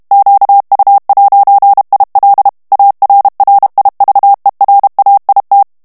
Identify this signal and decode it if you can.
Morse code, heard as a keyed tone, about 32 wpm, 800 Hz. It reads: QU'IL ARRIVERAIT
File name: FRANCAIS TEXTE CLAIR 2 VITESSE 30 MOTS MINUTES